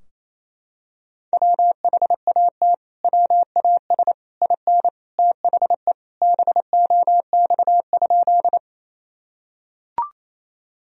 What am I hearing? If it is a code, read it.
Morse code, heard as a keyed tone, about 28 wpm, 700 Hz. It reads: WHAT WAS IN THE BOX?
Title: Sentences from Top 1000 Words - Review Part 2 - 28wpm